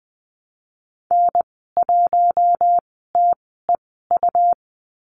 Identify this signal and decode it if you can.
Morse code, heard as a keyed tone, about 20 wpm, spaced as for 14 wpm, 700 Hz. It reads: N1TEU
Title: 154 Lesson 25g - Letter p - Call Signs 20wpm